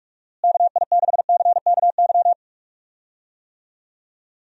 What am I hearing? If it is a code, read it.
Morse code, heard as a keyed tone, about 41 wpm, 700 Hz. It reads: KI6CKY
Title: USA Call Signs - 6 Characters - Repeated 3x - 45wpm